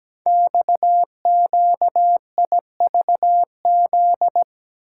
Morse code, heard as a keyed tone, about 17 wpm, 700 Hz. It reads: XQIVZ
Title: Code Groups - 5 Letters - 17wpm